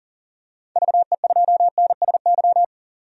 Morse code, heard as a keyed tone, about 40 wpm, 700 Hz. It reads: VE2NSY
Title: DXCC-100 Call Signs - 6 Characters - 40wpm